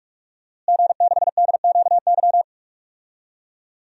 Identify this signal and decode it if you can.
Morse code, heard as a keyed tone, about 46 wpm, 700 Hz. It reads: G6DQY
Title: DXCC-100 Call Signs - 5 Characters - Repeated 2x - 45wpm